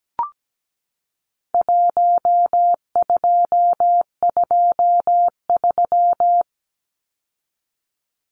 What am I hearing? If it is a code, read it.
Morse code, heard as a keyed tone, about 17 wpm, 700 Hz. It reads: 1223
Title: Numbers - Contest Style - 4-Digit 17wpm